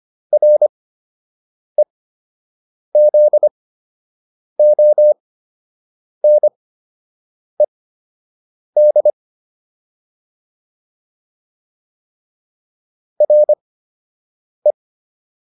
Morse code, heard as a keyed tone, about 25 wpm, spaced as for 7 wpm, 600 Hz. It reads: REZONED RE